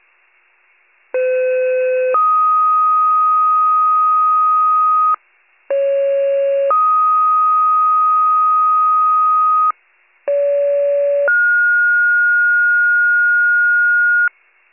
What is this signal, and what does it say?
Two-tone page, heard as a one-second tone, then a three-second tone, about 510 then 1250 Hz, about 550 then 1250 Hz, about 570 then 1430 Hz. It is heard from 1.1 to 14.3 s.